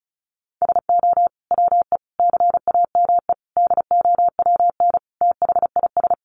Morse code, heard as a keyed tone, about 34 wpm, 700 Hz. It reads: SO WE CAME DOWN THIS
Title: Sentences from Top 400 Words 35wpm